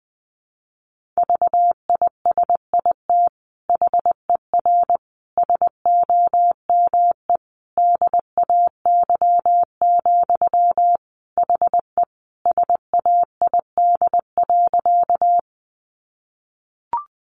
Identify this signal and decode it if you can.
Morse code, heard as a keyed tone, about 19 wpm, 700 Hz. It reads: VISIT HER SOME DAY, HE SAID.
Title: Sentences from Top 900 Words 20wpm